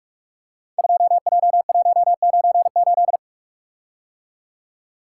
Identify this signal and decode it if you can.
Morse code, heard as a keyed tone, about 44 wpm, 700 Hz. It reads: 2J198
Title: Code Groups - 5 Letter:Numbers - 45wpm